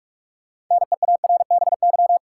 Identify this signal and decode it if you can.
Morse code, heard as a keyed tone, about 46 wpm, 700 Hz. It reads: NEARBY
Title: Six Letter Words - 45wpm